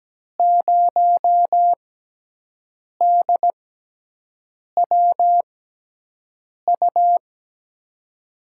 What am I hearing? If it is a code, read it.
Morse code, heard as a keyed tone, about 17 wpm, spaced as for 6 wpm, 700 Hz. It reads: 0DWU
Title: USA Call Signs - 6 Characters - 6x Character Spacing - 17wpm